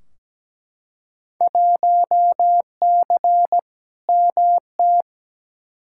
Morse code, heard as a keyed tone, about 17 wpm, 700 Hz. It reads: 1C MT
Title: Field Day Exchanges Encore - 17wpm